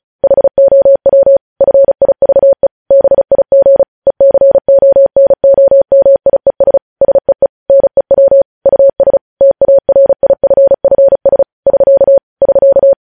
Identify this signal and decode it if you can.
Morse code, heard as a keyed tone, about 36 wpm, 550 Hz. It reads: HOW FIVE BIG ECONOMIES SEE NEW US TARIFFS <SK> <SK>
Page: news headlines in morse code at 35 wpm